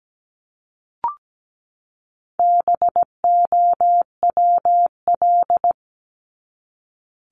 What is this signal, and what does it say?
Morse code, heard as a keyed tone, about 17 wpm, 700 Hz. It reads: BOWL
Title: Top 1001-1250 Words 17wpm